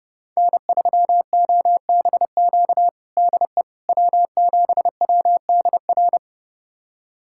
Morse code, heard as a keyed tone, about 30 wpm, 700 Hz. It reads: N3OBQ DE W7WDR
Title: USA Callsign DE Callsign - 5 Characters - 30wpm